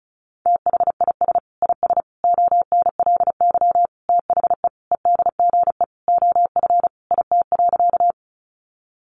Morse code, heard as a keyed tone, about 34 wpm, 700 Hz. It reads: THIS IS ONLY THE EDGE OF IT.
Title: Sentences from Top 1300 Words 35wpm